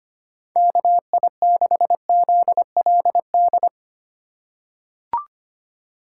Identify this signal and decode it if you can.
Morse code, heard as a keyed tone, about 25 wpm, 700 Hz. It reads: KI6ZLD
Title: USA Call Signs - 6 Characters - 4x Character Spacing - 25wpm